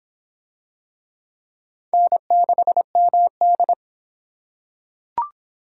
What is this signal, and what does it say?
Morse code, heard as a keyed tone, about 24 wpm, 700 Hz. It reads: N6MD